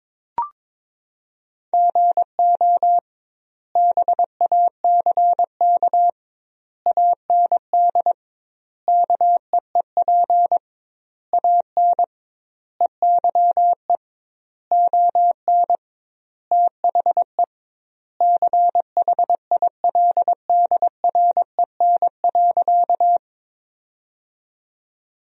Morse code, heard as a keyed tone, about 21 wpm, 700 Hz. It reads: GO BACK AND KEEP AN EYE ON THE CHILDREN.
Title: 2x Word-Spacing - Sentences from Top 300 Words - 22wpm